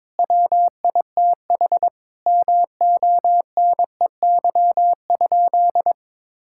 Morse code, heard as a keyed tone, about 22 wpm, 700 Hz. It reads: WITH MONEY?
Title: Sentences from Top 1600 Words 22wpm